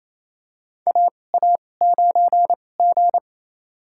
Morse code, heard as a keyed tone, about 28 wpm, spaced as for 20 wpm, 700 Hz. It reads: AA9G
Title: USA Call Signs - 4 Characters - 2x Character Spacing - 28wpm